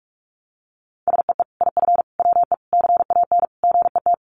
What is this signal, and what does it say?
Morse code, heard as a keyed tone, about 44 wpm, 700 Hz. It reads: SEE IF WE CAN GET
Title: Sets of 5 Words Encore2 45wpm